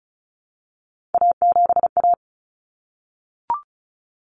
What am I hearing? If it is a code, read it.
Morse code, heard as a keyed tone, about 34 wpm, 700 Hz. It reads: A7A
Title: Common Cut Numbers - 3-Digit 35wpm